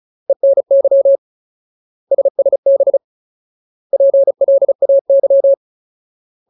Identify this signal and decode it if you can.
Morse code, heard as a keyed tone, about 34 wpm, 550 Hz. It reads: ENY SSB PLAY
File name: Ultimate Word List / 35WPM Ultimate